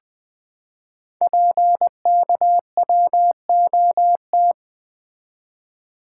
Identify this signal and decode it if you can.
Morse code, heard as a keyed tone, about 20 wpm, 700 Hz. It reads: PKWOT
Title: Code Groups - 5 Letters - 20wpm